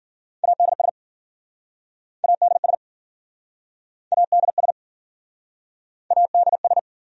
Morse code, heard as a keyed tone, about 52 wpm, 700 Hz. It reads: ADS ADS ADS ADS